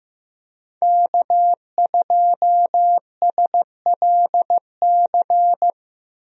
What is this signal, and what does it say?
Morse code, heard as a keyed tone, about 16 wpm, 700 Hz. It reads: K2SLC
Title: USA Call Signs - 5 Characters - 15wpm